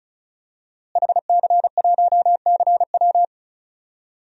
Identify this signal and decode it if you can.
Morse code, heard as a keyed tone, about 36 wpm, 700 Hz. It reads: HC1CW
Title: DXCC-100 Call Signs - 5 Characters - Repeated 3x - 35wpm